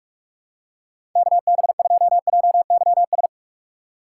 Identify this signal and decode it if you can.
Morse code, heard as a keyed tone, about 46 wpm, 700 Hz. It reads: K B 2 J Y S
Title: USA Call Signs - 6 Characters - 6x Character Spacing - 45wpm